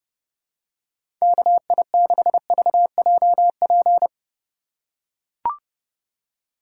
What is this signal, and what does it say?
Morse code, heard as a keyed tone, about 30 wpm, 700 Hz. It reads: KI6VJP
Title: USA Call Signs - 6 Characters - 2x Character Spacing - 30wpm